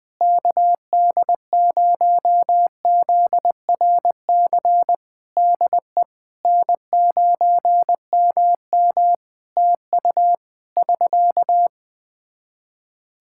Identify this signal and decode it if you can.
Morse code, heard as a keyed tone, about 21 wpm, 700 Hz. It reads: KD0ZRC DE N9MM TU <SK>
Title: Practice QSO - Exchange 4 - Ending - 20wpm